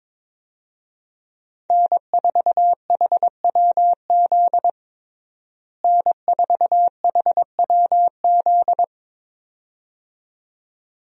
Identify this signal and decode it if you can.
Morse code, heard as a keyed tone, about 22 wpm, 700 Hz. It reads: N4HWZ N4HWZ
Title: USA Call Signs - 5 Characters - Repeated 2x - 22wpm